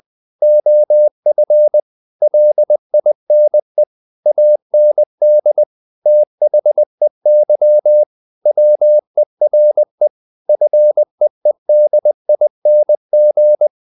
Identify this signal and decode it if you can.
Morse code, heard as a keyed tone, about 20 wpm, 600 Hz. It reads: OF LINE AND THEY WERE FEEDING
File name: the old man and the sea part 2 20wpm